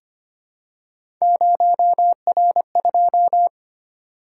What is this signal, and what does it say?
Morse code, heard as a keyed tone, about 25 wpm, 700 Hz. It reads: 0R2
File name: Three Letter-Number Combinations - 25wpm